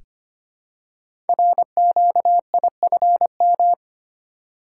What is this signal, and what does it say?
Morse code, heard as a keyed tone, about 25 wpm, 700 Hz. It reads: RQIFM
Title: Code Groups - 5 Letters - 25wpm